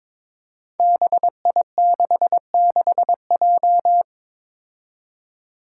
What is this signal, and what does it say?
Morse code, heard as a keyed tone, about 22 wpm, 700 Hz. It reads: BI66J
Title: Five Challenging Letter-Number Combinations 22wpm